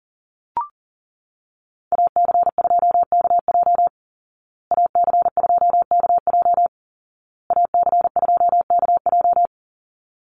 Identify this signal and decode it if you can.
Morse code, heard as a keyed tone, about 37 wpm, 700 Hz. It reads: AC2KJ AC2KJ AC2KJ
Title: Call Signs Repeated 2x - 40wpm